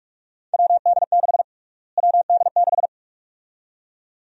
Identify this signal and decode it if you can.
Morse code, heard as a keyed tone, about 47 wpm, 700 Hz. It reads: WD6 WD6
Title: USA Call Sign Prefixes - Repeated 2x - 45wpm